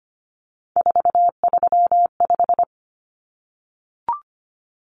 Morse code, heard as a keyed tone, about 25 wpm, 700 Hz. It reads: 435
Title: Common Cut Numbers - 3-Digit 25wpm